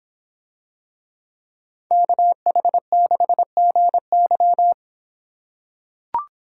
Morse code, heard as a keyed tone, about 26 wpm, 700 Hz. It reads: KH6GY